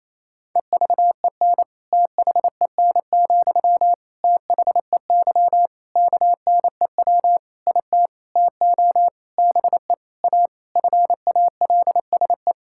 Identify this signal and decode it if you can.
Morse code, heard as a keyed tone, about 28 wpm, 700 Hz. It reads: EVEN THEN, THEY KNEW IT TO BE A FALSE
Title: Sentences from Top 1600 Words 28wpm